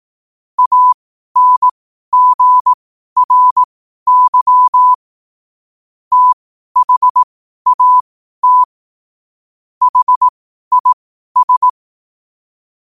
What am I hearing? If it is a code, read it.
Morse code, heard as a keyed tone, about 18 wpm, spaced as for 12 wpm, 1000 Hz. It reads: ANGRY THAT HIS
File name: qotdfiles / 12wpm